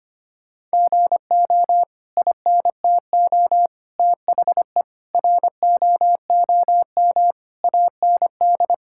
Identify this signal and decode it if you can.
Morse code, heard as a keyed tone, about 25 wpm, 700 Hz. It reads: GO INTO THE ROOM AND
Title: Sentences from Top 1000 Words - Review Part 6 - 25wpm